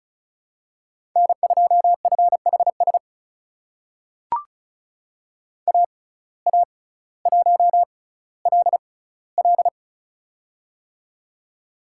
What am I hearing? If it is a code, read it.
Morse code, heard as a keyed tone, about 34 wpm, spaced as for 18 wpm, 700 Hz. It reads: N2FHS AA1LL
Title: USA Call Signs - 5 Characters - 6x Character Spacing - 35wpm